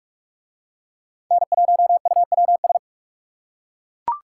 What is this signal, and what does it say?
Morse code, heard as a keyed tone, about 38 wpm, 700 Hz. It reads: N1UWS